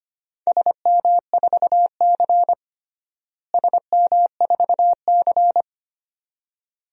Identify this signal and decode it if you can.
Morse code, heard as a keyed tone, about 25 wpm, 700 Hz. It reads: SM4C SM4C
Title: DXCC-100 Call Signs - 4 Characters - Repeated 2x - 25wpm